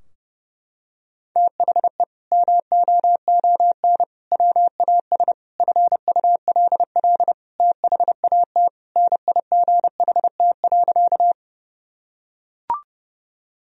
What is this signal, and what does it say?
Morse code, heard as a keyed tone, about 30 wpm, 700 Hz. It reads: THE MOON WAS FULL THAT NIGHT.
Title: Sentences from Top 1000 Words - Review Part 3 - 30wpm